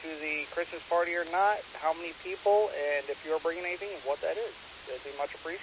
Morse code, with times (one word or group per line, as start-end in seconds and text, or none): none